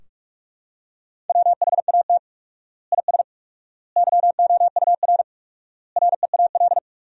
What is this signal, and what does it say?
Morse code, heard as a keyed tone, about 41 wpm, 700 Hz. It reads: WHAT IS YOUR REAL
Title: 4x Word-Spacing - Sentences from Top 300 Words - 45wpm